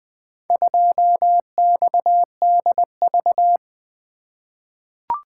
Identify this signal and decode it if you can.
Morse code, heard as a keyed tone, about 20 wpm, 700 Hz. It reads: 2XDV
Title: Four Letter-Number Combinations - 20wpm